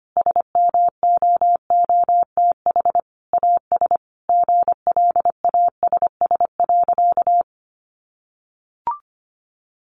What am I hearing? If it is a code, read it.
Morse code, heard as a keyed tone, about 25 wpm, 700 Hz. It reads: SMOOTH AS GLASS.
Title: Sentences from Top 1400 Words 25wpm